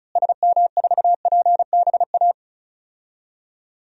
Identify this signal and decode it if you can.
Morse code, heard as a keyed tone, about 36 wpm, 700 Hz. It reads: SM4PBA
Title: DXCC-100 Call Signs - 6 Characters - Repeated 3x - 35wpm